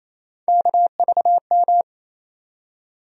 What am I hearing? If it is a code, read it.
Morse code, heard as a keyed tone, about 28 wpm, 700 Hz. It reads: KVM
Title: USA Call Sign Postfixes - 3 Letters 28wpm